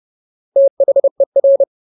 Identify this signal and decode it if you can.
Morse code, heard as a keyed tone, about 30 wpm, 550 Hz. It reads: THER
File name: Ultimate Word List / 30WPM Ultimate